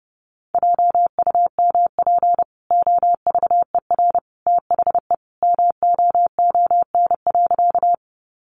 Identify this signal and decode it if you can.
Morse code, heard as a keyed tone, about 30 wpm, 700 Hz. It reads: JUMP OVER THE MOON.
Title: Sentences from Top 1300 Words 30wpm